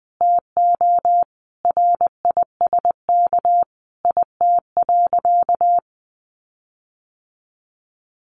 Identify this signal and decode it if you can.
Morse code, heard as a keyed tone, about 19 wpm, 700 Hz. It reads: TO RISK IT.
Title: Sentences from Top 1500 Words 20wpm